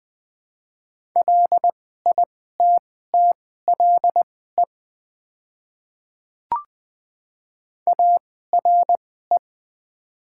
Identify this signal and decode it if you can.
Morse code, heard as a keyed tone, about 20 wpm, spaced as for 14 wpm, 700 Hz. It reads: LITTLE ARE
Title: 2x Character Spacing - Top 200 Words - 20wpm